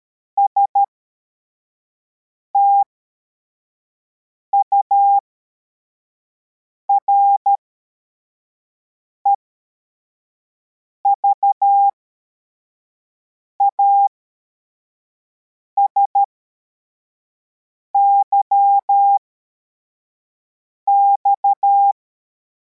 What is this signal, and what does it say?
Morse code, heard as a keyed tone, about 13 wpm, spaced as for 4 wpm, 800 Hz. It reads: STURE VASYX